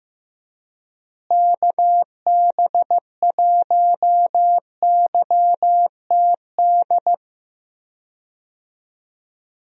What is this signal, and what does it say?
Morse code, heard as a keyed tone, about 15 wpm, 700 Hz. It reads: KB1YTD
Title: USA Call Signs - 6 Characters - Repeated 2x - 15wpm